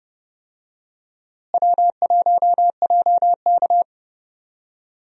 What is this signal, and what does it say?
Morse code, heard as a keyed tone, about 30 wpm, 700 Hz. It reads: W1JK
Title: USA Call Signs - 4 Characters - Repeated 3x - 30wpm